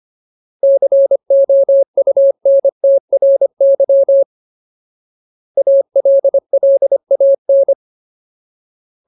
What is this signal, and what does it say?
Morse code, heard as a keyed tone, about 25 wpm, 550 Hz. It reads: COUNTRY ALLAN
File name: Ultimate Word List / 25WPM Ultimate